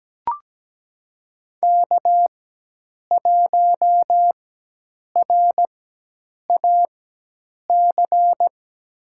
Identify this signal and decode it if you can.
Morse code, heard as a keyed tone, about 17 wpm, spaced as for 8 wpm, 700 Hz. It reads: K1RAC
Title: USA Call Signs - 5 Characters - 4x Character Spacing - 17wpm